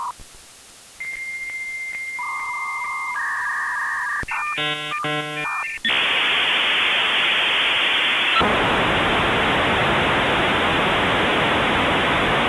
Dial-up modem handshake.